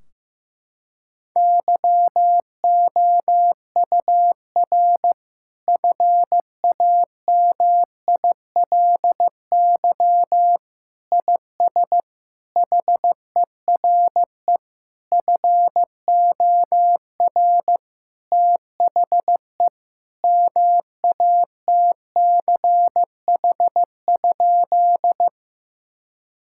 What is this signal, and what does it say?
Morse code, heard as a keyed tone, about 16 wpm, 700 Hz. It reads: YOUR FAMILY IS HERE FOR THE MATCH?
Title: Sentences from Top 1600 Words 15wpm